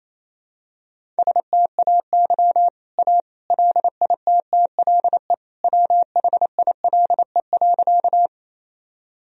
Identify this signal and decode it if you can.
Morse code, heard as a keyed tone, about 27 wpm, 700 Hz. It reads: STAY A LITTLE WHILE.
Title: Sentences from Top 1000 Words - Review Part 1 - 28wpm